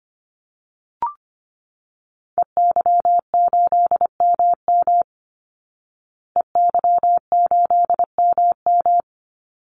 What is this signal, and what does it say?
Morse code, heard as a keyed tone, about 25 wpm, 700 Hz. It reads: EY8MM EY8MM
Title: Call Signs Repeated 3x - 25wpm